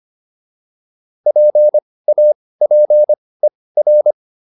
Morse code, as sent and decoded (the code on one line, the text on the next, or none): .--. .- .--. .-.
PAPER